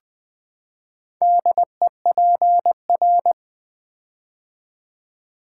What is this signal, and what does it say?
Morse code, heard as a keyed tone, about 20 wpm, 700 Hz. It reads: DEPR
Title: Build A Word - 4 Letters At A Time - 10-Letter Words - 20wpm